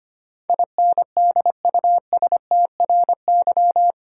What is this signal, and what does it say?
Morse code, heard as a keyed tone, about 25 wpm, 700 Hz. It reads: INDUSTRY
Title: Sentences from Top 1000 Words - Review Part 4 - New Word Drill - 25wpm